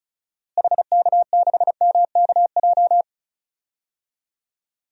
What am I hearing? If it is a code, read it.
Morse code, heard as a keyed tone, about 36 wpm, 700 Hz. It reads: HK6MKJ
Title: DXCC-100 Call Signs - 6 Characters - Repeated 3x - 35wpm